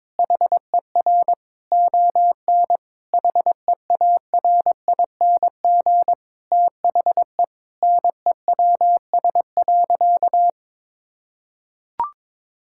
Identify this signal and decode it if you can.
Morse code, heard as a keyed tone, about 22 wpm, 700 Hz. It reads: HER ON HEARING THE NEWS.
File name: Difficult Sentences with Hints - 22wpm